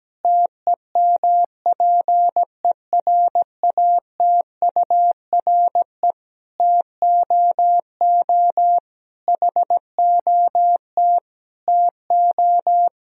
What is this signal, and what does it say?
Morse code, heard as a keyed tone, about 17 wpm, 700 Hz. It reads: TEMPERATURE TOO HOT TO
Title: Sentences from Top 1000 Words - Review Part 4 - 17wpm